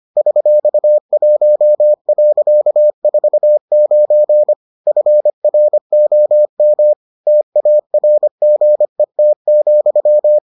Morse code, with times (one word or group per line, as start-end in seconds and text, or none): 0.2-4.5 $1.49
4.9-6.9 FROM
7.3-10.4 TARGET,